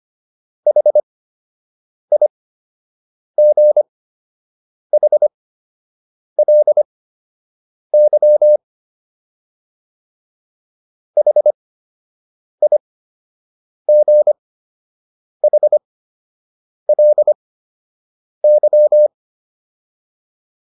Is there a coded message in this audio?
Yes: Morse code from 0.7 to 19.1 s, heard as a keyed tone, about 25 wpm, spaced as for 7 wpm, 600 Hz. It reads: HIGHLY HIGHLY